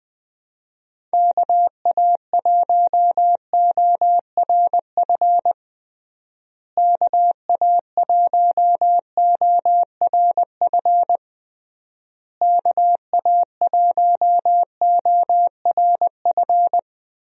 Morse code, heard as a keyed tone, about 20 wpm, 700 Hz. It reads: KA1ORF KA1ORF KA1ORF
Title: USA Call Signs - 6 Characters - Repeated 3x - 20wpm